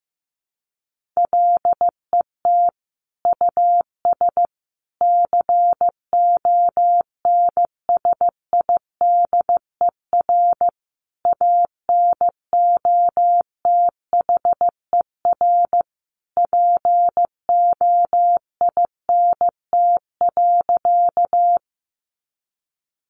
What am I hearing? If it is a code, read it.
Morse code, heard as a keyed tone, about 15 wpm, 700 Hz. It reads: LET US CONSIDER ANOTHER POINT.
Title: Sentences from Top 600 Words 15wpm